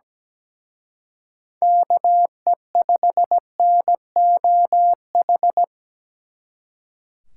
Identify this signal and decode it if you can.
Morse code, heard as a keyed tone, about 20 wpm, 700 Hz. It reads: KE5NOH
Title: Speed Racing - USA 2x3 Call Signs 17wpm